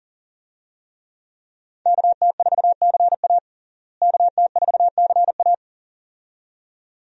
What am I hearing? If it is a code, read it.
Morse code, heard as a keyed tone, about 40 wpm, 700 Hz. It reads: KT4CA KT4CA
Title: USA Call Signs - 5 Characters - Repeated 2x - 40wpm